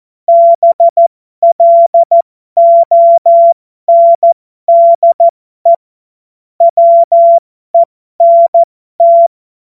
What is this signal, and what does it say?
Morse code, heard as a keyed tone, about 14 wpm, spaced as for 12 wpm, 680 Hz. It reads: BLONDE WENT
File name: Morsekurs / Witze